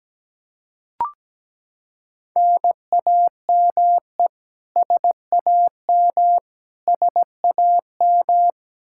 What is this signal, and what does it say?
Morse code, heard as a keyed tone, about 17 wpm, 700 Hz. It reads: NAME SAM SAM
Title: Practice QSO - Exchange 1 - Beginning - 17wpm